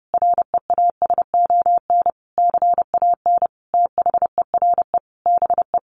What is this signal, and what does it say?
Morse code, heard as a keyed tone, about 30 wpm, 700 Hz. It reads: REASON CAN THERE BE
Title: Sentences from Top 1300 Words 30wpm